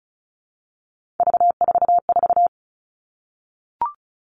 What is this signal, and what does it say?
Morse code, heard as a keyed tone, about 34 wpm, 700 Hz. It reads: V44
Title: All Cut Numbers - 3-Digit 35wpm